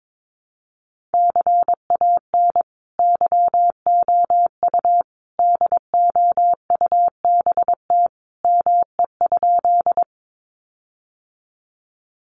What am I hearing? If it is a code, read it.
Morse code, heard as a keyed tone, about 22 wpm, 700 Hz. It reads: CAN YOU DOUBT ME?
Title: Sentences from Top 1500 Words 22wpm